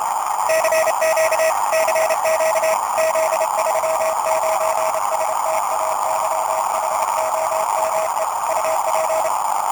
Morse code, heard as a keyed tone, about 29 wpm, 640 Hz, with noise. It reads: CQ CQ Z39 STAYHOME UP